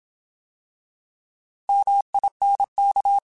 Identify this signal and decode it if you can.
Morse code, heard as a keyed tone, about 27 wpm, 800 Hz. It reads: MINK
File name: Alphabetical / Alphabetical mp3s / Ms @25